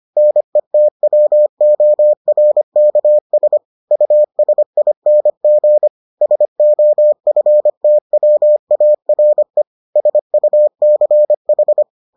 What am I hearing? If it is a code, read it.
Morse code, heard as a keyed tone, about 25 wpm, 600 Hz. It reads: NETWORKS USING SOFTWARE SUCH